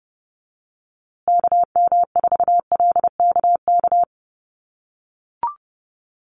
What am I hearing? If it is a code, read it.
Morse code, heard as a keyed tone, about 30 wpm, 700 Hz. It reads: KM4LKK
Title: USA Call Signs Repeated 2x 30wpm